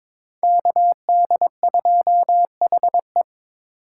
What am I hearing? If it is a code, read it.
Morse code, heard as a keyed tone, about 22 wpm, 700 Hz. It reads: KD2HE
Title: K1USN SST QSO Practice - 22wpm